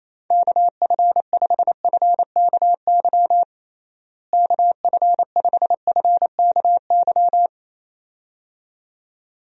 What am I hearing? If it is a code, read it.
Morse code, heard as a keyed tone, about 28 wpm, 700 Hz. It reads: KF5FKY KF5FKY
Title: USA Call Signs - 6 Characters - Repeated 2x - 28wpm